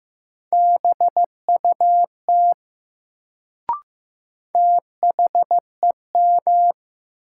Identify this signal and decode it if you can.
Morse code, heard as a keyed tone, about 15 wpm, 700 Hz. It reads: BUT THEM
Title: Sentences from Top 100 Words - New Word Drill - 15wpm